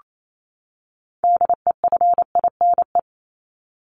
Morse code, heard as a keyed tone, about 28 wpm, 700 Hz. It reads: DEFINE
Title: ED Suffix - Top 100 Words with Base Words 28wpm